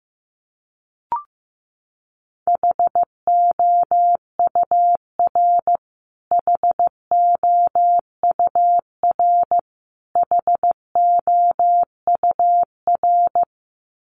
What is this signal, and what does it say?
Morse code, heard as a keyed tone, about 15 wpm, 700 Hz. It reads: HOUR HOUR HOUR
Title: Top 500 Words Repeated 15wpm